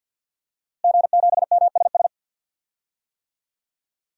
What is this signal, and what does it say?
Morse code, heard as a keyed tone, about 50 wpm, 700 Hz. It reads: G7MSS
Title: DXCC-100 Call Signs - 5 Characters - Repeated 2x - 50wpm